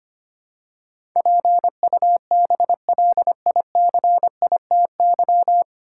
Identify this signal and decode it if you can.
Morse code, heard as a keyed tone, about 25 wpm, 700 Hz. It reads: PUBLICITY
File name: practice / 25wpm / 9 Letter Words - First 3 Letters Given - 25wpm